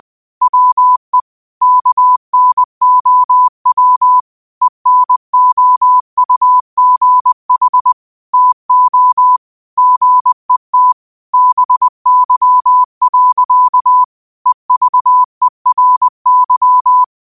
Morse code, heard as a keyed tone, about 20 wpm, 1000 Hz. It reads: WE KNOW ENOUGH TO GET BY. EVERY
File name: 20wpm